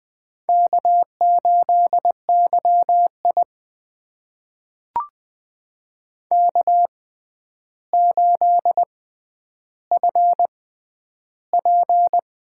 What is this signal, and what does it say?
Morse code, heard as a keyed tone, about 20 wpm, spaced as for 10 wpm, 700 Hz. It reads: K8YI K8FP
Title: USA Call Signs - 4 Characters - 6x Character Spacing - 20wpm